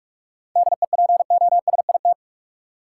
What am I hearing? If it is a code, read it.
Morse code, heard as a keyed tone, about 46 wpm, 700 Hz. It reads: DEPOSIT
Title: Sentences from Top 1300 Words - New Word Drill Alt - 45wpm